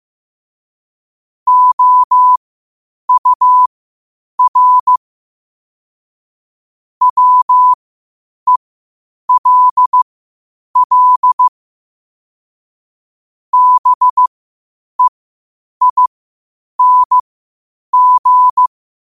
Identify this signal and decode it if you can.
Morse code, heard as a keyed tone, about 15 wpm, spaced as for 8 wpm, 1000 Hz. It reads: OUR WELL BEING
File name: qotdfiles / 8wpm